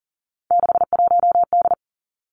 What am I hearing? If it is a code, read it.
Morse code, heard as a keyed tone, about 40 wpm, 700 Hz. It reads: 61D
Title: Three Challenging Letter-Number Combinations 40wpm